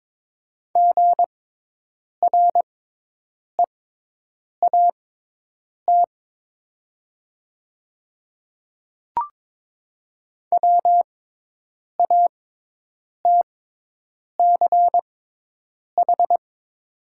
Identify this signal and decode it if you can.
Morse code, heard as a keyed tone, about 22 wpm, spaced as for 8 wpm, 700 Hz. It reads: GREAT WATCH
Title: 6x Character Spacing - Top 300 Words - 22wpm